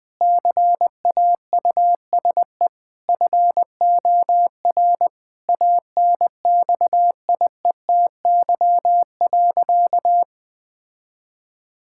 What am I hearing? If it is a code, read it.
Morse code, heard as a keyed tone, about 20 wpm, 700 Hz. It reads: CAUSE FOR ANXIETY.
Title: Sentences from Top 1600 Words 20wpm